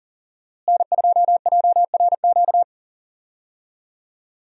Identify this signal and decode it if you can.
Morse code, heard as a keyed tone, about 40 wpm, 700 Hz. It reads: N2JRQ
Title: USA Call Signs - 5 Characters - Repeated 3x - 40wpm